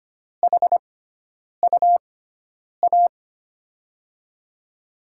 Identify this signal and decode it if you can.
Morse code, heard as a keyed tone, about 25 wpm, spaced as for 9 wpm, 700 Hz. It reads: HUA
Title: USA Call Signs - 6x Character Spacing - 25wpm